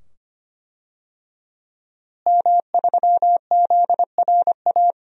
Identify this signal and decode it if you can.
Morse code, heard as a keyed tone, about 25 wpm, 700 Hz. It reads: M3ZRA
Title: DXCC-100 Call Signs - 5 Characters - Repeated 2x - 25wpm